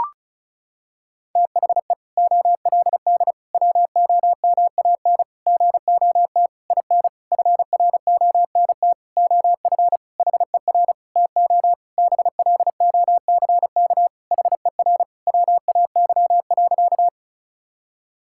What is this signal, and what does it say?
Morse code, heard as a keyed tone, about 36 wpm, 700 Hz. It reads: THE OLD WOMAN GOT IN FRONT OF HER TO BLOCK HER WAY.
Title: Sentences from Top 1600 Words 35wpm